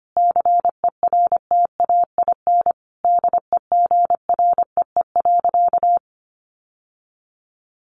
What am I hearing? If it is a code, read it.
Morse code, heard as a keyed tone, about 25 wpm, 700 Hz. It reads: CERTAIN DEGREE.
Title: Sentences from Top 1500 Words 25wpm